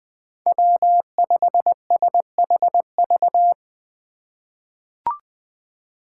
Morse code, heard as a keyed tone, about 20 wpm, 700 Hz. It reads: W5SHV
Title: USA Call Signs - 5 Characters - 2x Character Spacing - 20wpm